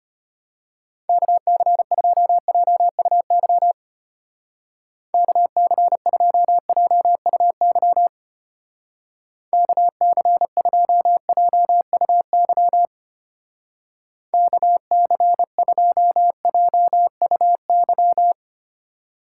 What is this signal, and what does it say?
Morse code, heard as a keyed tone, about 31 wpm, 700 Hz. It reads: KC2JUY KC2JUY KC2JUY KC2JUY
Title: Speed Racing - USA 2x3 Call Signs 25wpm